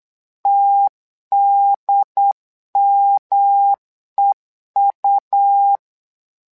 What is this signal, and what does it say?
Morse code, heard as a keyed tone, about 8 wpm, 800 Hz. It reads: TDMEU
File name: LEZIONI / 3 MWRO / 3 MWRO 8wpm 2606